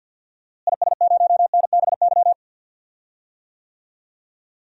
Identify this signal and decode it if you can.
Morse code, heard as a keyed tone, about 52 wpm, 700 Hz. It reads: IS0NBY
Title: DXCC-100 Call Signs - 6 Characters - Repeated 3x - 50wpm